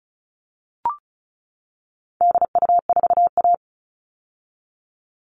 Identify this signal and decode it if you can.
Morse code, heard as a keyed tone, about 34 wpm, 700 Hz. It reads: DU4A
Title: All Cut Numbers - 4-Digit 35wpm